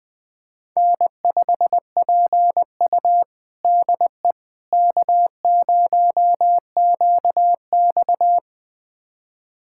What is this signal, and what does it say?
Morse code, heard as a keyed tone, about 20 wpm, 700 Hz. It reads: N5PU DE K0QX
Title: USA Callsign DE Callsign - 4 Characters - 20wpm